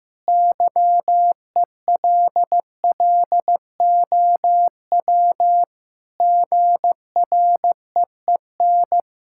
Morse code, heard as a keyed tone, about 15 wpm, 700 Hz. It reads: YELLOW GREEN
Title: Pangram Challenge v2 - 15wpm